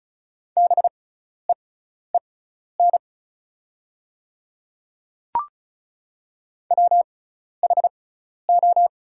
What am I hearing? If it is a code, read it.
Morse code, heard as a keyed tone, about 36 wpm, spaced as for 12 wpm, 700 Hz. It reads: BEEN WHO
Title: 6x Character Spacing - Top 100 Words - 35wpm